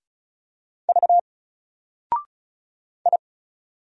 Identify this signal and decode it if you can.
Morse code, heard as a keyed tone, about 36 wpm, 700 Hz. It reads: V I
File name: Single-Letter-Number-35wpm